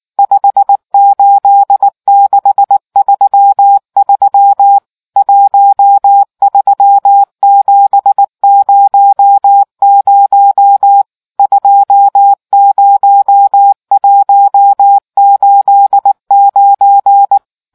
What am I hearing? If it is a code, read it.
Morse code, heard as a keyed tone, about 19 wpm, 800 Hz. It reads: 58633 13700 20189